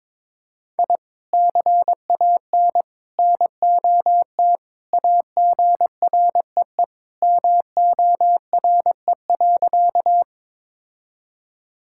Morse code, heard as a keyed tone, about 22 wpm, 700 Hz. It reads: I CAN NOT AGREE MORE.
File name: practice / 22wpm / Sentences with 3-Word Phrases - 22wpm